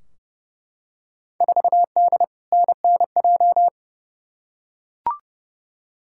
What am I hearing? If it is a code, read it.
Morse code, heard as a keyed tone, about 30 wpm, 700 Hz. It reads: 4D NNJ
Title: Field Day Exchanges Encore - 30wpm